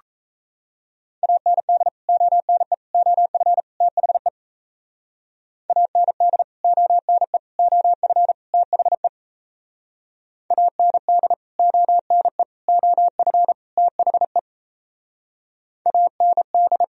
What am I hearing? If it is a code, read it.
Morse code, heard as a keyed tone, about 33 wpm, 690 Hz. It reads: AND ONE OF THE AND ONE OF THE AND ONE OF THE AND
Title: Speed-Racing - Sets of 4 Words 28wpm